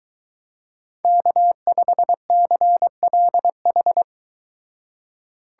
Morse code, heard as a keyed tone, about 23 wpm, 700 Hz. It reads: K5CLH